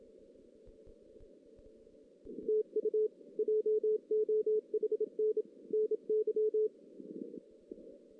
Morse code, heard as a keyed tone, about 30 wpm, 440 Hz. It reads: TU JOHN NY
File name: 2016 JAN CW NAQP / 2016 Jan naqp k5ot - 0-4